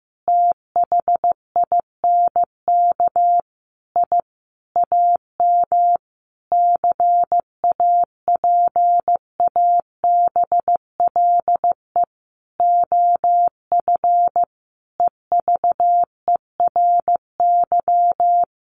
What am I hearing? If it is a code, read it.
Morse code, heard as a keyed tone, about 15 wpm, 700 Hz. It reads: THINK I AM CAPABLE OF EVERY
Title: Sentences from Top 1200 Words 15wpm